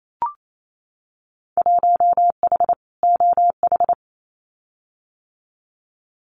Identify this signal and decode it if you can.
Morse code, heard as a keyed tone, about 28 wpm, 700 Hz. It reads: 1H OH
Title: Winter Field Day Exchanges 28wpm